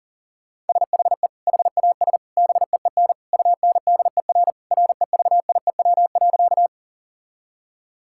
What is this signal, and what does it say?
Morse code, heard as a keyed tone, about 40 wpm, 700 Hz. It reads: SHE HAS BEEN UNDER REVIEW.
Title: Sentences from Top 1000 Words - Review Part 6 - 40wpm